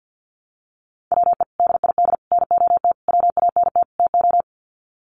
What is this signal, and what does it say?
Morse code, heard as a keyed tone, about 50 wpm, 700 Hz. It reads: WE DID NOT WANT TO
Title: Sets of 5 Words Encore 50wpm